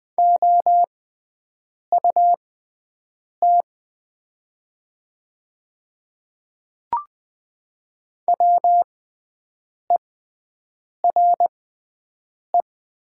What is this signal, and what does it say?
Morse code, heard as a keyed tone, about 20 wpm, spaced as for 7 wpm, 700 Hz. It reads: OUT WERE
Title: 6x Character Spacing - Top 100 Words - 20wpm